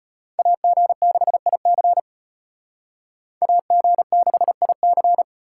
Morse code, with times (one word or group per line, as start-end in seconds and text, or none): 0.4-2.0 AG6IC
3.4-5.2 AG6IC